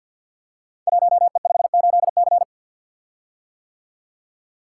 Morse code, heard as a keyed tone, about 50 wpm, 700 Hz. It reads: 1E58C
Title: Code Groups - 5 Letter:Numbers - 50wpm